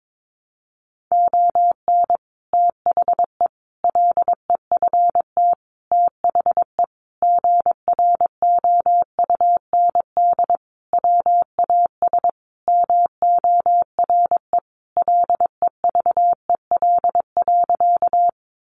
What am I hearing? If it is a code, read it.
Morse code, heard as a keyed tone, about 22 wpm, 700 Hz. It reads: ON THE LEFT THE GROUND WAS MORE LEVEL.
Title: Sentences from Top 700 Words 22wpm